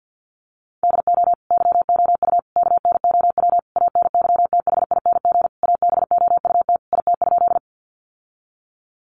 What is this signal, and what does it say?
Morse code, heard as a keyed tone, about 50 wpm, 700 Hz. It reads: DO YOU KNOW ANYTHING ABOUT IT?
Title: Sentences from Top 1000 Words 50wpm